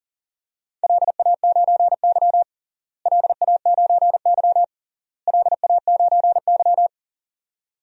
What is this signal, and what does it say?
Morse code, heard as a keyed tone, about 37 wpm, 700 Hz. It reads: LA9Y LA9Y LA9Y
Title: DXCC-100 Call Signs - 4 Characters - Repeated 3x - 40wpm